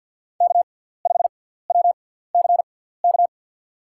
Morse code, heard as a keyed tone, about 53 wpm, 700 Hz. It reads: K 5 W C K
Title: USA Call Signs - 6x Character Spacing - 50wpm